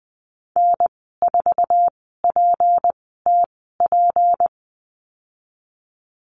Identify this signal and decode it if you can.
Morse code, heard as a keyed tone, about 19 wpm, 700 Hz. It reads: N 4 P T P
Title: 154 Lesson 25g - Letter p - Call Signs 20wpm